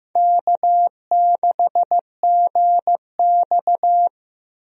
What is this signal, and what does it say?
Morse code, heard as a keyed tone, about 15 wpm, 700 Hz. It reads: K6GX
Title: USA Call Signs - 4 Characters - Repeated 2x - 15wpm